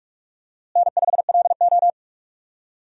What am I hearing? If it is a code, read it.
Morse code, heard as a keyed tone, about 43 wpm, 700 Hz. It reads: N5LO
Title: USA Call Signs - 4 Characters - Repeated 2x - 45wpm